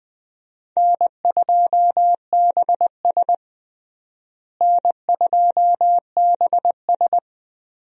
Morse code, heard as a keyed tone, about 20 wpm, 700 Hz. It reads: N2BS N2BS
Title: USA Call Signs - 4 Characters - Repeated 3x - 20wpm